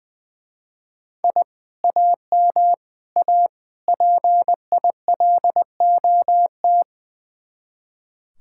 Morse code, heard as a keyed tone, about 20 wpm, 700 Hz. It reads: I AM A PILOT
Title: Speed-Racing Common QSO Phrases Encore - 20wpm